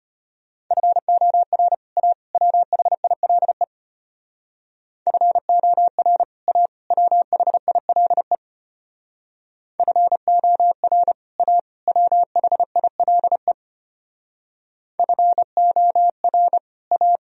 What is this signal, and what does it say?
Morse code, heard as a keyed tone, about 33 wpm, 700 Hz. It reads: FOR A WHILE FOR A WHILE FOR A WHILE FOR A